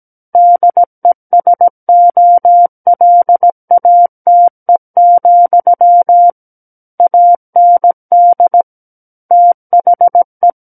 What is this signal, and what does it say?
Morse code, heard as a keyed tone, about 17 wpm, 700 Hz. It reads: DESOLATE, AND THE
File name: Book One / 08-Chapter 8 - 17 WPM